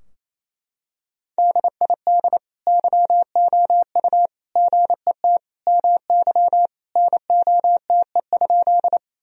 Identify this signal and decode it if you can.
Morse code, heard as a keyed tone, about 28 wpm, 700 Hz. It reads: DID YOU GET MY NOTE?
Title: Sentences from Top 1000 Words - Review Part 2 - 28wpm